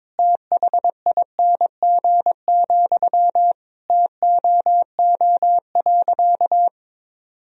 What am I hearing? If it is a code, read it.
Morse code, heard as a keyed tone, about 22 wpm, 700 Hz. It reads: THING, TOO.
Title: Sentences from Top 200 Words and Abbreviations 22wpm